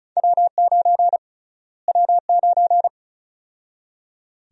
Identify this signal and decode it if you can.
Morse code, heard as a keyed tone, about 36 wpm, 700 Hz. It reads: W9 W9
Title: USA Call Sign Prefixes - Repeated 2x - 35wpm